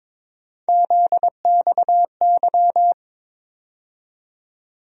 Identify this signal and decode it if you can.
Morse code, heard as a keyed tone, about 21 wpm, 700 Hz. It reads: ZXY
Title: USA Call Sign Postfixes - 3 Letters 22wpm